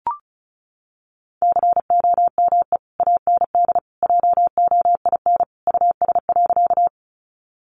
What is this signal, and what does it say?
Morse code, heard as a keyed tone, about 34 wpm, 700 Hz. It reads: COME AND JOIN US.
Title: Sentences from Top 1400 Words 35wpm